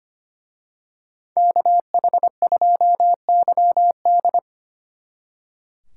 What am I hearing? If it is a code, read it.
Morse code, heard as a keyed tone, about 29 wpm, 700 Hz. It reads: KH2YD
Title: Speed Racing - USA 2x2 Call Signs 25wpm